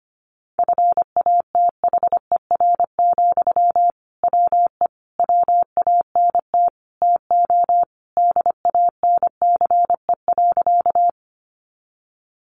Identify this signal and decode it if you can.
Morse code, heard as a keyed tone, about 25 wpm, 700 Hz. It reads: FATHER, WE WANT TO DANCE.
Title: Sentences from Top 700 Words 25wpm